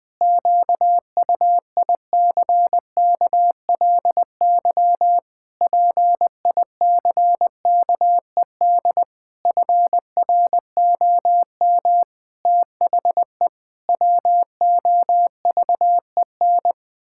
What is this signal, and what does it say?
Morse code, heard as a keyed tone, about 20 wpm, 700 Hz. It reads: QUICKLY PICKED FROM THE WOVEN
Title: Pangram Challenge v2 - 20wpm